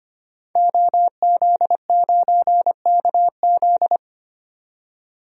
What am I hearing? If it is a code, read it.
Morse code, heard as a keyed tone, about 25 wpm, 690 Hz. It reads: OZ9KZ
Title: DXCC-100 Call Signs - 5 Characters - 25wpm